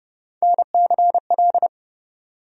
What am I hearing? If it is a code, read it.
Morse code, heard as a keyed tone, about 31 wpm, 700 Hz. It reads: NCL
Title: USA Call Sign Postfixes - 3 Letters 30wpm